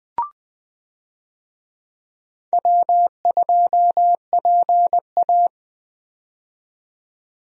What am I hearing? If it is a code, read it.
Morse code, heard as a keyed tone, about 20 wpm, 700 Hz. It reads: W2PA
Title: USA Call Signs - 4 Characters - 20wpm